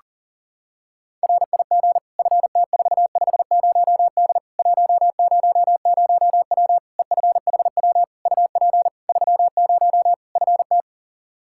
Morse code, heard as a keyed tone, about 40 wpm, 700 Hz. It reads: RIG FT450D 100W EFHW UP 30 FT
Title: K1USN SST QSO Practice 40wpm